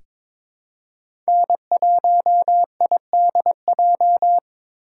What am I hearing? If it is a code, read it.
Morse code, heard as a keyed tone, about 22 wpm, 700 Hz. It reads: N1IDJ
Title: USA Call Signs - 5 Characters - 4x Character Spacing - 22wpm